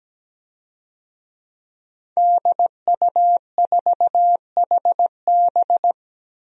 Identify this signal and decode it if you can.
Morse code, heard as a keyed tone, about 17 wpm, 700 Hz. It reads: DU4HB
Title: DXCC-100 Call Signs - 5 Characters - Repeated 3x - 17wpm